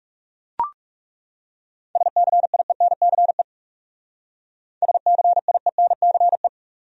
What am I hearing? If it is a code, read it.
Morse code, heard as a keyed tone, about 39 wpm, 700 Hz. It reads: SCIENCE SCIENCE